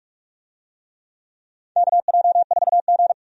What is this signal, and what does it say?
Morse code, heard as a keyed tone, about 42 wpm, 700 Hz. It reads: KJ4G
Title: USA Call Signs - 4 Characters - 45wpm